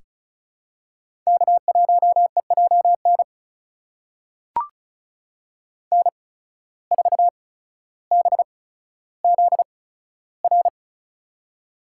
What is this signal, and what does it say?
Morse code, heard as a keyed tone, about 36 wpm, spaced as for 15 wpm, 700 Hz. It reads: K1EJN N4BZR
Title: USA Call Signs - 5 Characters - 8x Character Spacing - 35wpm